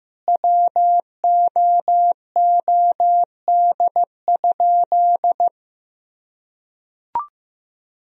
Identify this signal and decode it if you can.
Morse code, heard as a keyed tone, about 16 wpm, 700 Hz. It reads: WOOD?
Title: Sentences from Top 1000 Words - Review Part 2 - 15wpm